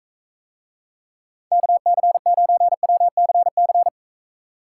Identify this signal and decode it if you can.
Morse code, heard as a keyed tone, about 43 wpm, 700 Hz. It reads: KC9WCC